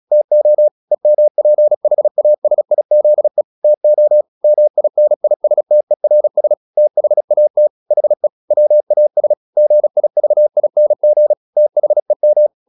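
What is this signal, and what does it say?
Morse code, heard as a keyed tone, about 36 wpm, 600 Hz. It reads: TO EMPHASIZE TO MINISTERS THAT HE WAS GIVING THEM
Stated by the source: Morse practice at 36 wpm, text of Ken Follett's novel Never, from CW med Gnister.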